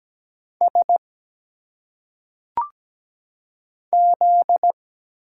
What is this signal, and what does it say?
Morse code, heard as a keyed tone, about 17 wpm, 700 Hz. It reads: S Z
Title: Single Letters - Phonetic Pronunciation 17wpm